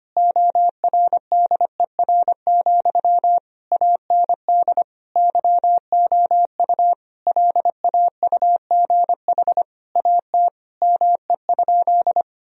Morse code, heard as a keyed tone, about 25 wpm, 700 Hz. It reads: ORDER, AND YOU LAUGH AT ME?
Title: Sentences from Top 1000 Words - Review Part 2 - 25wpm